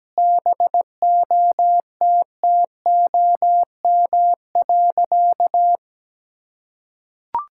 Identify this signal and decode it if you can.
Morse code, heard as a keyed tone, about 17 wpm, 700 Hz. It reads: BOTTOM.
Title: Sentences from Top 1000 Words - Review Part 6 - 17wpm